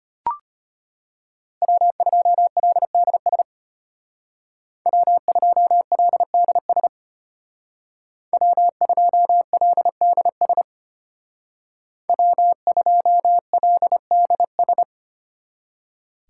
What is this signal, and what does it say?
Morse code, heard as a keyed tone, about 33 wpm, 700 Hz. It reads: W2LDS W2LDS W2LDS W2LDS